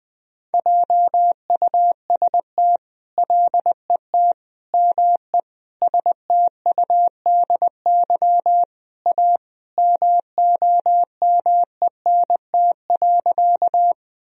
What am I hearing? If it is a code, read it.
Morse code, heard as a keyed tone, about 20 wpm, 700 Hz. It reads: JUST LET ME STUDY A MOMENT.
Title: Sentences from Top 1000 Words - Review Part 1 - 20wpm